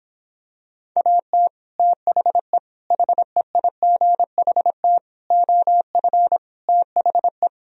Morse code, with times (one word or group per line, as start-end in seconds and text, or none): 1.0-1.5 AT
1.8-2.6 THE
2.9-5.0 HEIGHT
5.3-6.4 OF
6.7-7.5 THE